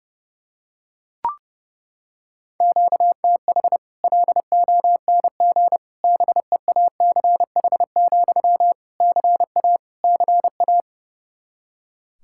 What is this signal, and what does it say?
Morse code, heard as a keyed tone, about 30 wpm, 700 Hz. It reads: QTH LONG BEACH, CA CA
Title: Practice QSO - Exchange 1 - Beginning - 30wpm